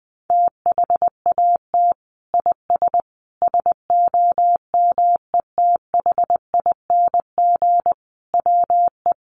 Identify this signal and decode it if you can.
Morse code, heard as a keyed tone, about 20 wpm, 700 Hz. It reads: THAT IS SOMETHING WE
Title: Sentences from Top 1100 Words 20wpm